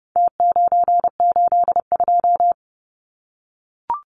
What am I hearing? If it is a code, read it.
Morse code, heard as a keyed tone, about 30 wpm, 700 Hz. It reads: T982
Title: Numbers - Contest Style - 4-Digit 30wpm